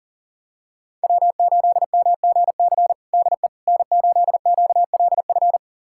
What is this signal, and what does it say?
Morse code, heard as a keyed tone, about 37 wpm, 700 Hz. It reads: W8MGC DE N8QLF
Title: USA Callsign DE Callsign - 5 Characters - 40wpm